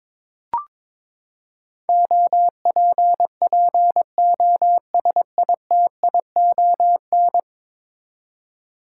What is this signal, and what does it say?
Morse code, heard as a keyed tone, about 22 wpm, 700 Hz. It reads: OPPOSITION
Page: morse-code-ninja-practice-22wpm